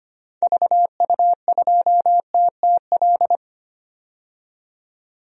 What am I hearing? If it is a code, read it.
Morse code, heard as a keyed tone, about 25 wpm, 700 Hz. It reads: VU2TTL
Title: DXCC-100 Call Signs - 6 Characters - 25wpm